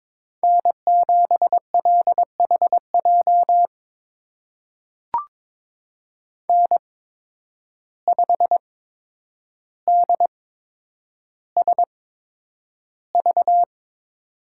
Morse code, heard as a keyed tone, about 22 wpm, spaced as for 9 wpm, 700 Hz. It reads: N7LHJ N5DSV
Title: USA Call Signs - 5 Characters - 8x Character Spacing - 22wpm